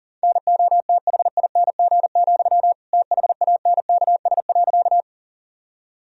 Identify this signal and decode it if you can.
Morse code, heard as a keyed tone, about 40 wpm, 700 Hz. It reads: NOTHING, THANKS.
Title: Sentences from Top 1000 Words - Review Part 3 - 40wpm